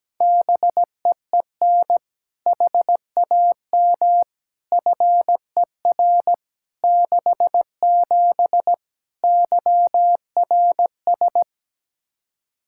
Morse code, heard as a keyed tone, about 17 wpm, 700 Hz. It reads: BEEN HAM FER 67 YRS
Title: Practice QSO - Exchange 3 - More Info - 17wpm